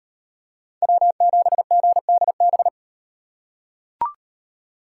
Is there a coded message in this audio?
Yes: Morse code from 0.8 to 2.7 s, heard as a keyed tone, about 37 wpm, 700 Hz. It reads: W7GDB